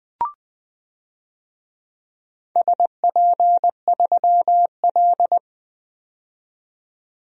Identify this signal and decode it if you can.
Morse code, heard as a keyed tone, about 20 wpm, 700 Hz. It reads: SP3L
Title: DXCC-100 Call Signs - 4 Characters - 20wpm